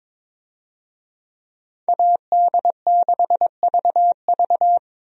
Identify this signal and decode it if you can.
Morse code, heard as a keyed tone, about 22 wpm, 700 Hz. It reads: AD6VV
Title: USA Call Signs - 5 Characters - 22wpm